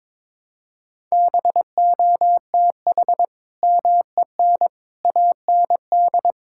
Morse code, heard as a keyed tone, about 22 wpm, 700 Hz. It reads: BOTH MEN AND
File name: practice / 22wpm / Sentences with 3-Word Phrases - 22wpm